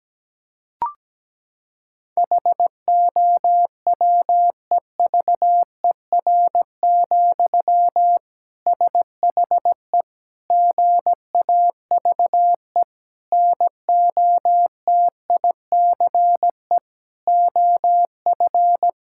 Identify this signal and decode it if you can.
Morse code, heard as a keyed tone, about 17 wpm, 700 Hz. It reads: HOWEVER, SHE GAVE NOTICE OF
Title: Sentences from Top 1000 Words - Review Part 6 - 17wpm